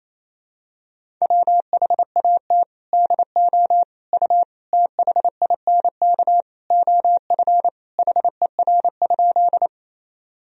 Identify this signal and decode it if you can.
Morse code, heard as a keyed tone, about 28 wpm, 700 Hz. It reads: WHAT DO U THINK OF HER?
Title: Sentences from Top 200 Words and Abbreviations 28wpm